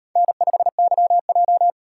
Morse code, heard as a keyed tone, about 34 wpm, 700 Hz. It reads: N5YJ